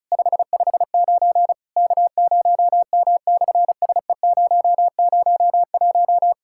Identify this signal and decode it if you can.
Morse code, heard as a keyed tone, about 33 wpm, 700 Hz. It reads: 559 K0M/SE001
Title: SOTA QSO Practice 35wpm